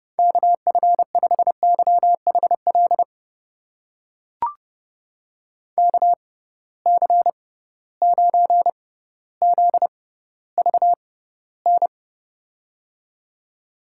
Morse code, heard as a keyed tone, about 30 wpm, spaced as for 15 wpm, 700 Hz. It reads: KF5YHL KC9ZVN
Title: USA Call Signs - 6 Characters - 6x Character Spacing - 30wpm